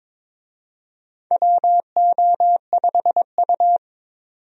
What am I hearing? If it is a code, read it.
Morse code, heard as a keyed tone, about 22 wpm, 700 Hz. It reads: WO5U
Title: USA Call Signs - 4 Characters - 22wpm